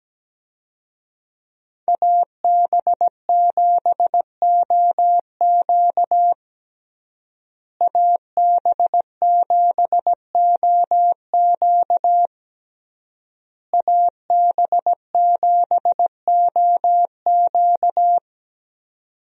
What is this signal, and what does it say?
Morse code, heard as a keyed tone, about 17 wpm, 700 Hz. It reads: AB7OQ AB7OQ AB7OQ
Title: USA Call Signs - 5 Characters - Repeated 3x - 17wpm